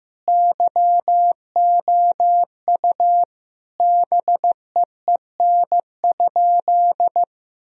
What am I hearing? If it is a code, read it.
Morse code, heard as a keyed tone, about 16 wpm, 700 Hz. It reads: YOU BEEN?
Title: Sentences from Top 1000 Words - Review Part 3 - 15wpm